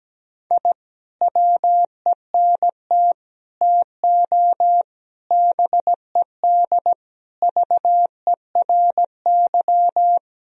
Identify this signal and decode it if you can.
Morse code, heard as a keyed tone, about 17 wpm, 700 Hz. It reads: I WENT TO BED VERY
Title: Sentences from Top 1000 Words - Review Part 2 - 17wpm